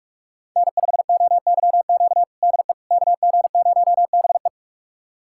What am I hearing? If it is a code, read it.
Morse code, heard as a keyed tone, about 46 wpm, 690 Hz. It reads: N5OYQ DE KG0BE
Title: USA Callsign DE Callsign 45wpm